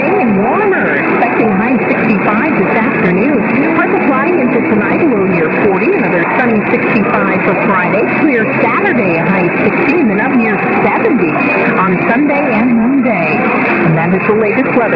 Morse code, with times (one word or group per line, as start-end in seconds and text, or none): none